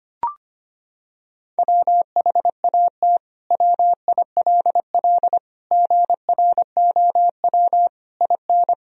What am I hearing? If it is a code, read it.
Morse code, heard as a keyed tone, about 25 wpm, 700 Hz. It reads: WHAT WILL GROW IN
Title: Sentences from Top 1000 Words - Review Part 2 - 25wpm